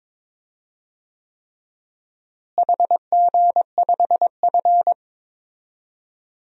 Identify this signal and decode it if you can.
Morse code, heard as a keyed tone, about 22 wpm, 700 Hz. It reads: HG5F
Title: DXCC-100 Call Signs - 4 Characters - Repeated 3x - 22wpm